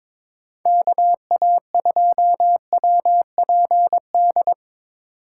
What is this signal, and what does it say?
Morse code, heard as a keyed tone, about 22 wpm, 700 Hz. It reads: KA2WPD
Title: USA Call Signs - 6 Characters - 6x Character Spacing - 22wpm